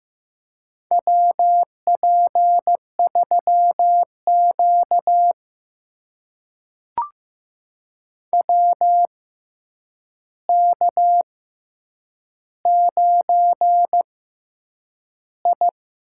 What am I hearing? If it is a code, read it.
Morse code, heard as a keyed tone, about 15 wpm, spaced as for 8 wpm, 700 Hz. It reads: WP3Q WK9I
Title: USA Call Signs - 4 Characters - 6x Character Spacing - 15wpm